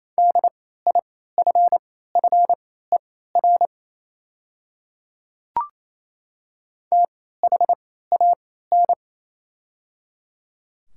Morse code, heard as a keyed tone, about 28 wpm, spaced as for 16 wpm, 700 Hz. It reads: DIFFER THAN
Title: 3x Character Spacing - Top 200 Words - 28wpm